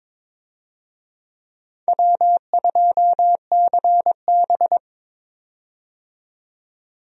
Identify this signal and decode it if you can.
Morse code, heard as a keyed tone, about 22 wpm, 700 Hz. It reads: W2CB
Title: USA Call Signs - 4 Characters - 22wpm